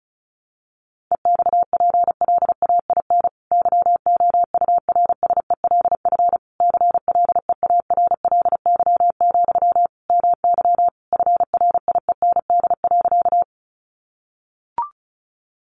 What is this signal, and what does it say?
Morse code, heard as a keyed tone, about 34 wpm, 700 Hz. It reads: EXPLAIN YOURSELF CLEARLY, MY FRIEND.
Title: Sentences from Top 1300 Words 35wpm